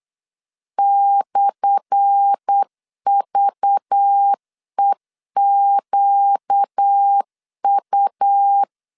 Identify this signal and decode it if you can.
Morse code, heard as a keyed tone, about 8 wpm, 800 Hz. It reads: /VEQU
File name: LEZIONI / 10 SIGNS / 10A SLASH 8wpm 3452